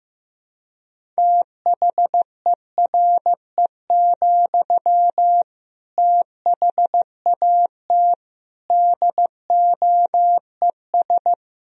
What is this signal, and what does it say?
Morse code, heard as a keyed tone, about 15 wpm, 700 Hz. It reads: THERE, THAT DOES
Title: Sentences from Top 200 Words and Abbreviations 15wpm